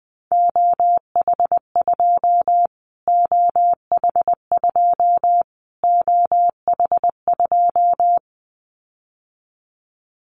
Morse code, heard as a keyed tone, about 20 wpm, 700 Hz. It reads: OH2 OH2 OH2
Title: Callsign Prefixes for Top 25 DXCC - Repeated 3x - 20wpm